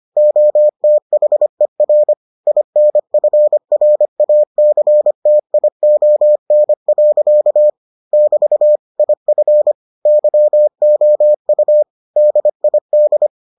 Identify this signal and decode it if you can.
Morse code, heard as a keyed tone, about 25 wpm, 600 Hz. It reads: OTHER INFRACTION. <BT> IF YOU DID